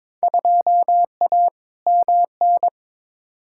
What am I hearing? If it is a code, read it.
Morse code, heard as a keyed tone, about 22 wpm, 700 Hz. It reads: 2A MN
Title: Field Day Exchanges - 22wpm